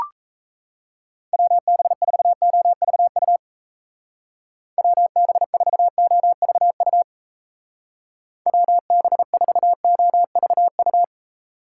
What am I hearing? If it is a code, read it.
Morse code, heard as a keyed tone, about 34 wpm, 700 Hz. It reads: WB4OVU WB4OVU WB4OVU
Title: Speed Racing - USA 2x3 Call Signs 28wpm